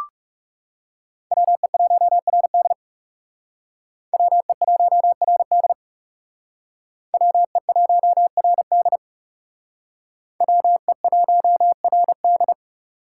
Morse code, heard as a keyed tone, about 37 wpm, 700 Hz. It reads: WE1RD WE1RD WE1RD WE1RD
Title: Speed Racing - USA 2x2 Call Signs 30wpm